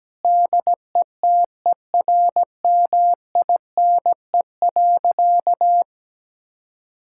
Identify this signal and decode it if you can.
Morse code, heard as a keyed tone, about 17 wpm, 700 Hz. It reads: DETERMINE.
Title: Sentences from Top 1000 Words - Review Part 5 - 17wpm